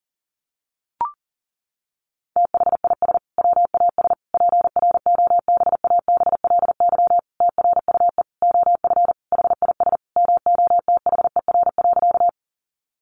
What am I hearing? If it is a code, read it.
Morse code, heard as a keyed tone, about 40 wpm, 700 Hz. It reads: THIS WAS PROBABLY TRUE OF HIS MOTHER.
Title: Sentences from Top 700 Words 40wpm